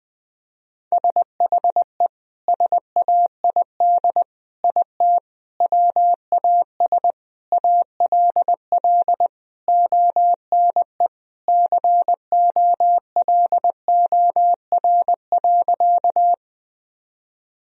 Morse code, heard as a keyed tone, about 20 wpm, 700 Hz. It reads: SHE SAID IT WAS ALL ONE COLOR.
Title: Sentences from Top 1000 Words - Review Part 2 - 20wpm